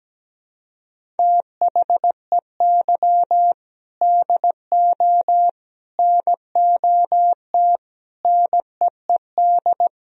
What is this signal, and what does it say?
Morse code, heard as a keyed tone, about 17 wpm, 700 Hz. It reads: THEY DO NOT NEED
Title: Sentences from Top 1000 Words - Review Part 1 - 17wpm